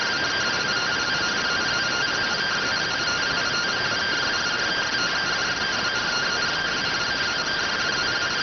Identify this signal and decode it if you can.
Two-tone teleprinter signal, about 170 Hz shift.